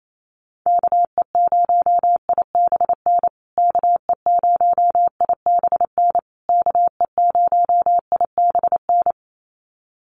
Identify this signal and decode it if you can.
Morse code, heard as a keyed tone, about 28 wpm, 700 Hz. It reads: KE0IBN KE0IBN KE0IBN
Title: USA Call Signs Repeated 3x 28wpm